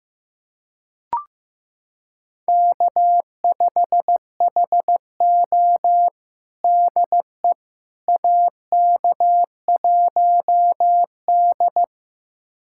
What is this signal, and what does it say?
Morse code, heard as a keyed tone, about 16 wpm, 700 Hz. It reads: K5HO DE AK1D
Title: USA Callsign DE Callsign - 4 Characters - 15wpm